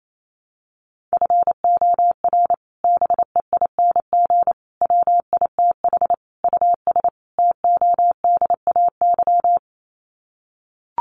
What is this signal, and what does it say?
Morse code, heard as a keyed tone, about 28 wpm, 700 Hz. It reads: FOR BEING WITH US TODAY
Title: Sets of 5 Words Encore2 28wpm